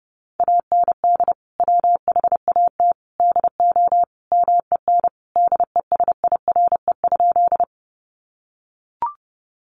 Morse code, heard as a keyed tone, about 30 wpm, 700 Hz. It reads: AND WHAT DO MEN DESIRE?
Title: Sentences from Top 1100 Words 30wpm